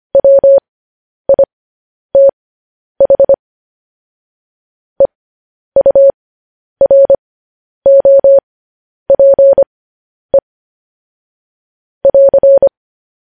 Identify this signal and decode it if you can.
Morse code, heard as a keyed tone, about 25 wpm, spaced as for 10 wpm, 550 Hz. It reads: WITH EUROPE <AR>
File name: podcast / News headlines-1x-25x10